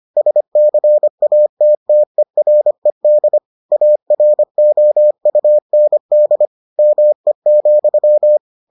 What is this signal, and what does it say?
Morse code, heard as a keyed tone, about 25 wpm, 600 Hz. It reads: SCATTERED AROUND ME,